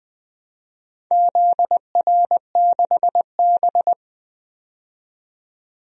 Morse code, heard as a keyed tone, about 20 wpm, 700 Hz. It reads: ZR6B
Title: DXCC-100 Call Signs - 4 Characters - 20wpm